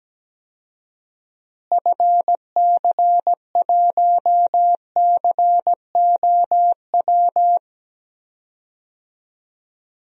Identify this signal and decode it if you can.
Morse code, heard as a keyed tone, about 17 wpm, 700 Hz. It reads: FC1COW
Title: DXCC-100 Call Signs - 6 Characters - Repeated 3x - 17wpm